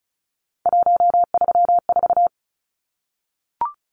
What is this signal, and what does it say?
Morse code, heard as a keyed tone, about 34 wpm, 700 Hz. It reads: 134
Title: Numbers - 3-Digit 35wpm